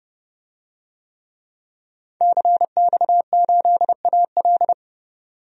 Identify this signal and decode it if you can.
Morse code, heard as a keyed tone, about 29 wpm, 700 Hz. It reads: CX8AL
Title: DXCC-100 Call Signs - 5 Characters - 30wpm